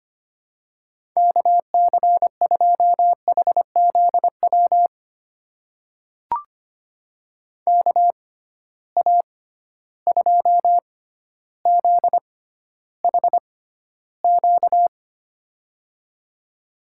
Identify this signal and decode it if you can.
Morse code, heard as a keyed tone, about 25 wpm, spaced as for 13 wpm, 700 Hz. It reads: KC2HZW KA2ZHQ
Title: USA Call Signs - 6 Characters - 6x Character Spacing - 25wpm